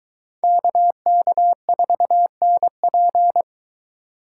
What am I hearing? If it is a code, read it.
Morse code, heard as a keyed tone, about 21 wpm, 700 Hz. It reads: KK4NP